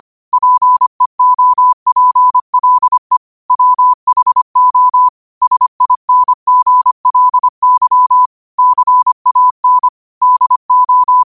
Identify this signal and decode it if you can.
Morse code, heard as a keyed tone, about 25 wpm, 1000 Hz. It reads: PEOPLE WHO SINGLY CAN DO
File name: qotdfiles / 25wpm